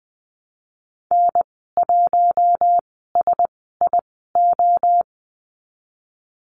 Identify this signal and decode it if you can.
Morse code, heard as a keyed tone, about 19 wpm, spaced as for 14 wpm, 700 Hz. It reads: N1SIO
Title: 055 Lesson 10g - Letter r - Call Signs 20wpm